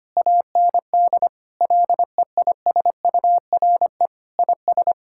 Morse code, heard as a keyed tone, about 25 wpm, 700 Hz. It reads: AND LEISURE IS
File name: practice / 25wpm / Difficult Sentences with Hints - 25wpm